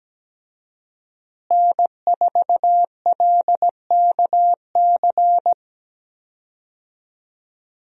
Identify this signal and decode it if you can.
Morse code, heard as a keyed tone, about 17 wpm, 700 Hz. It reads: N4LKC
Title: USA Call Signs - 5 Characters - Repeated 2x - 17wpm